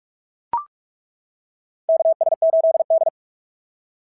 Morse code, heard as a keyed tone, about 43 wpm, 690 Hz. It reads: KS8D
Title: SOTA QSO Practice 45wpm